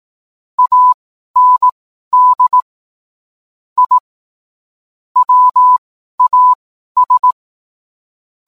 Morse code, heard as a keyed tone, about 18 wpm, spaced as for 12 wpm, 1000 Hz. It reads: AND I WAS